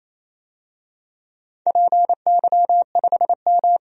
Morse code, heard as a keyed tone, about 28 wpm, 700 Hz. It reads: PY5M
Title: DXCC-100 Call Signs - 4 Characters - 28wpm